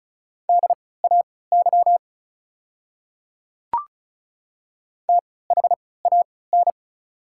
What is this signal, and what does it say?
Morse code, heard as a keyed tone, about 36 wpm, spaced as for 20 wpm, 700 Hz. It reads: DAY THAN
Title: 3x Character Spacing - Top 100 Words - 35wpm